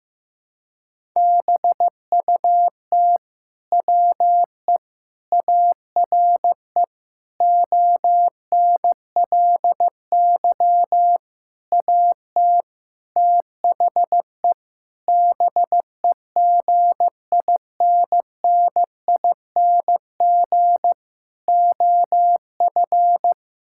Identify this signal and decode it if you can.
Morse code, heard as a keyed tone, about 16 wpm, 700 Hz. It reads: BUT WE ARE ONLY AT THE BEGINNING OF
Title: Sentences from Top 1000 Words - Review Part 5 - 15wpm